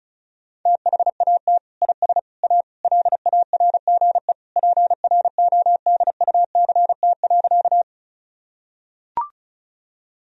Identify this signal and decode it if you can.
Morse code, heard as a keyed tone, about 34 wpm, 700 Hz. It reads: THAT IS A LARGE PRODUCT.
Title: Sentences from Top 1000 Words - Review Part 2 - 35wpm